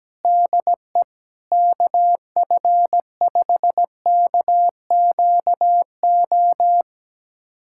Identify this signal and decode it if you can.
Morse code, heard as a keyed tone, about 17 wpm, 700 Hz. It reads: DE KF5KQO
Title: USA Callsign DE Callsign - 6 Characters - 17wpm